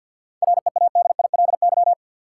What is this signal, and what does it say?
Morse code, heard as a keyed tone, about 52 wpm, 700 Hz. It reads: READILY
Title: Seven Letter Words - 50wpm